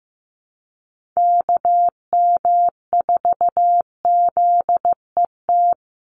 Morse code, heard as a keyed tone, about 15 wpm, 700 Hz. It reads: KM4ZET
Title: USA Call Signs 15wpm